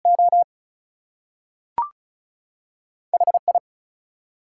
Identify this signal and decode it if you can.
Morse code, heard as a keyed tone, about 33 wpm, spaced as for 3 wpm, 700 Hz. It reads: OH I
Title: Two Letter Words - 35wpm